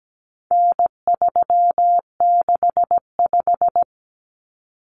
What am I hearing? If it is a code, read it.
Morse code, heard as a keyed tone, about 17 wpm, 700 Hz. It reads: N365
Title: Common Cut Numbers - 4-Digit 17wpm